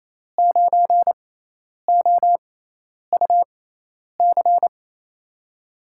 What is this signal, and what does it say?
Morse code, heard as a keyed tone, about 28 wpm, spaced as for 10 wpm, 700 Hz. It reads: 9OUC
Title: USA Call Signs - 6x Character Spacing - 28wpm